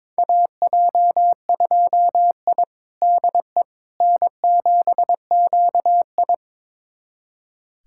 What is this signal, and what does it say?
Morse code, heard as a keyed tone, about 22 wpm, 700 Hz. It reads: AJ2I DE N7QI
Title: USA Callsign DE Callsign - 4 Characters - 22wpm